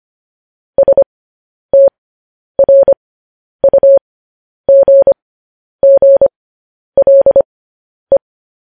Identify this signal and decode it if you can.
Morse code, heard as a keyed tone, about 25 wpm, spaced as for 10 wpm, 550 Hz. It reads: STRUGGLE
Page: news-headlines-in-morse-code-at-10-wpm